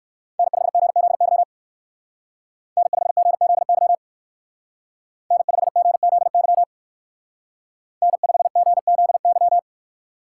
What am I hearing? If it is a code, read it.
Morse code, heard as a keyed tone, about 58 wpm, 700 Hz. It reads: N5GZY N5GZY N5GZY N5GZY